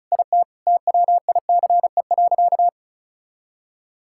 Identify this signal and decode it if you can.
Morse code, heard as a keyed tone, about 36 wpm, 700 Hz. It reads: IT TWICE.
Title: Sentences from Top 1600 Words 35wpm